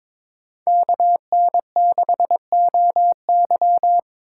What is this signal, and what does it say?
Morse code, heard as a keyed tone, about 26 wpm, 700 Hz. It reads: KN6OY
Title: Speed Racing - USA 2x2 Call Signs 22wpm